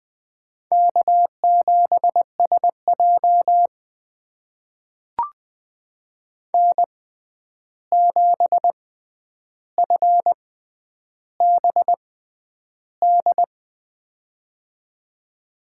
Morse code, heard as a keyed tone, about 21 wpm, spaced as for 10 wpm, 700 Hz. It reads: K7SJ N7FBD
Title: USA Call Signs - 6x Character Spacing - 20wpm